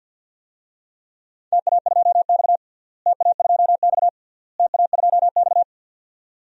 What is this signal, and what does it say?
Morse code, heard as a keyed tone, about 50 wpm, 700 Hz. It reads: TA2X TA2X TA2X
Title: DXCC-100 Call Signs - 4 Characters - Repeated 3x - 50wpm